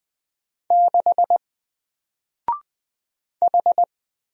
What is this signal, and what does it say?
Morse code, heard as a keyed tone, about 20 wpm, 700 Hz. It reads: 6 H
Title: Single Letter-Number Alt 20wpm